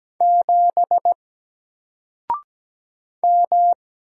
Morse code, heard as a keyed tone, about 17 wpm, 700 Hz. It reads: 7 M
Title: Single Letter-Number Alt 17wpm